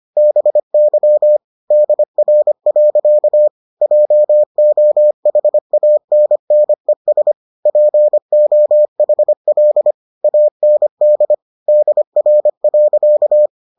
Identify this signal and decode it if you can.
Morse code, heard as a keyed tone, about 25 wpm, 600 Hz. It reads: BY DR. JOHANNES POHL AND DR.